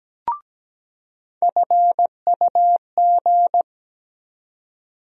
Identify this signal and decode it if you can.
Morse code, heard as a keyed tone, about 17 wpm, 700 Hz. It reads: FUG
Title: USA Call Sign Postfixes - 3 Letters 17wpm